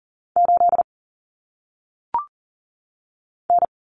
Morse code, heard as a keyed tone, about 40 wpm, 700 Hz. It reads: 8 N
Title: Common Cut Numbers - 1-Digit 40wpm